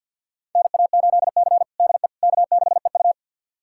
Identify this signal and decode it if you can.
Morse code, heard as a keyed tone, about 51 wpm, 700 Hz. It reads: NA8C DE K6EU
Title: USA Callsign DE Callsign - 4 Characters - 50wpm